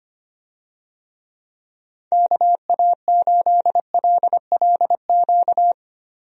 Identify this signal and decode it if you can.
Morse code, heard as a keyed tone, about 25 wpm, 700 Hz. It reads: KA8LLQ
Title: USA Call Signs - 6 Characters - Repeated 3x - 25wpm